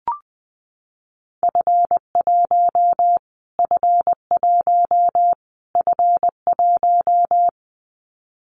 Morse code, heard as a keyed tone, about 20 wpm, 700 Hz. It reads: F1 F1 F1
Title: Callsign Prefixes for Top 25 DXCC - Repeated 3x - 20wpm